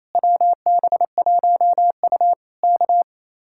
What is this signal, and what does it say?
Morse code, heard as a keyed tone, about 27 wpm, 700 Hz. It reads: WB1U K
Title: Speed-Racing Common QSO Phrases - 28wpm